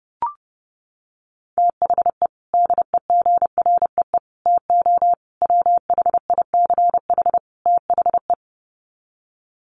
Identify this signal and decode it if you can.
Morse code, heard as a keyed tone, about 30 wpm, 700 Hz. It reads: THE DEGREE TO WHICH THE
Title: Sets of 5 Words Encore2 30wpm